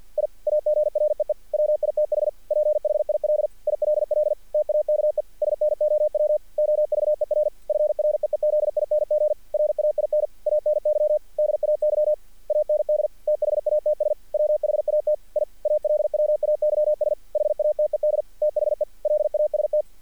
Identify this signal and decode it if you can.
Morse code, heard as a keyed tone, about 46 wpm, 610 Hz. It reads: I AGREE WITH PHIL ILL TAKE SNOW OVER FREEZING RAIN ANY DAY AND THATS WHAT I ALWAYS HATED THE LAST
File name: archived / CRSnet